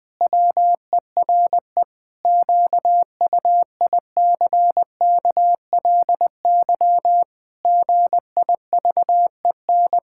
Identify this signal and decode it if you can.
Morse code, heard as a keyed tone, about 21 wpm, 700 Hz. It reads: WERE QUICKLY GIVEN
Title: Pangram Challenge v2 - 20wpm